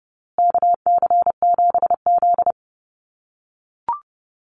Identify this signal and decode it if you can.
Morse code, heard as a keyed tone, about 30 wpm, 700 Hz. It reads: KC7Z
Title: USA Call Signs Repeated 3x 30wpm